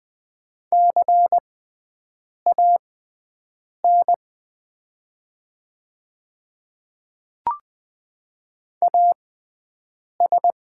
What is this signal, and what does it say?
Morse code, heard as a keyed tone, about 20 wpm, spaced as for 7 wpm, 690 Hz. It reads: CAN AS
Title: 6x Character Spacing - Top 200 Words - 20wpm